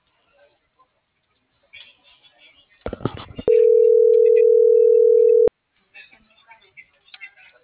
Ringback tone (North America), 440 + 480 Hz.